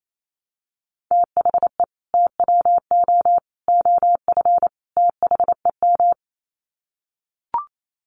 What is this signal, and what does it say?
Morse code, heard as a keyed tone, about 28 wpm, 700 Hz. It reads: THE TWO OF THEM
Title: Sets of 4 Words 28wpm